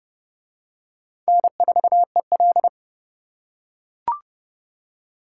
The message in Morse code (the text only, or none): N4EL